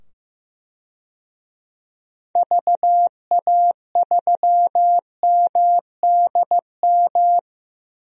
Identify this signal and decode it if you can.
Morse code, heard as a keyed tone, about 15 wpm, 700 Hz. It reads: VA3MDM
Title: DXCC-100 Call Signs - 6 Characters - 15wpm